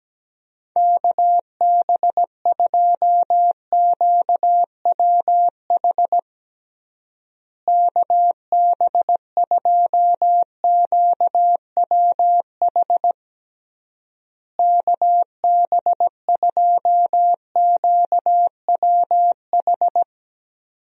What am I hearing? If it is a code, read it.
Morse code, heard as a keyed tone, about 17 wpm, 700 Hz. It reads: KB2QWH KB2QWH KB2QWH
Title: USA Call Signs - 6 Characters - Repeated 3x - 17wpm